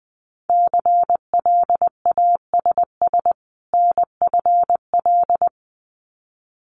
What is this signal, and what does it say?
Morse code, heard as a keyed tone, about 19 wpm, 700 Hz. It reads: CLASS NFL
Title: Winter Field Day Exchanges 20wpm